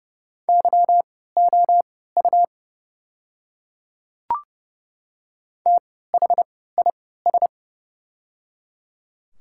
Morse code, heard as a keyed tone, about 30 wpm, spaced as for 17 wpm, 700 Hz. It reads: YOU THIS
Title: 3x Character Spacing - Top 100 Words - 30wpm